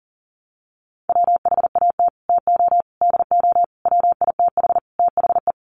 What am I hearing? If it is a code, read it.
Morse code, heard as a keyed tone, about 40 wpm, 700 Hz. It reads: WHAT TO DO WITH THE
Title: Sets of 5 Words Encore2 40wpm